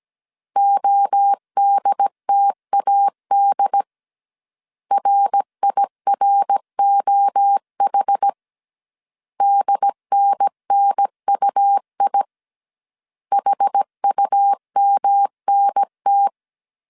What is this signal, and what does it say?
Morse code, heard as a keyed tone, about 17 wpm, 800 Hz. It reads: ODTAD RIROH DNNUI HUMNT